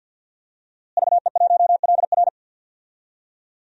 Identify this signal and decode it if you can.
Morse code, heard as a keyed tone, about 51 wpm, 700 Hz. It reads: VE1LR
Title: DXCC-100 Call Signs - 5 Characters - Repeated 2x - 50wpm